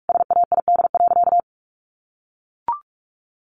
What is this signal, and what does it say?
Morse code, heard as a keyed tone, about 44 wpm, 700 Hz. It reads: SAID.
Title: Sentences from Top 1400 Words 45wpm